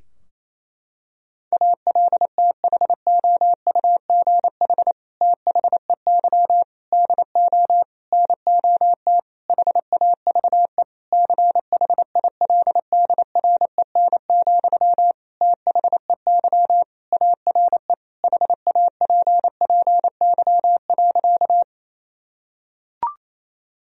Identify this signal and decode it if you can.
Morse code, heard as a keyed tone, about 27 wpm, 700 Hz. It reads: ALTHOUGH THEY DO NOT HAVE CHILDREN, THEY ARE HAPPY.
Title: Sentences from Top 1000 Words - Review Part 3 - 28wpm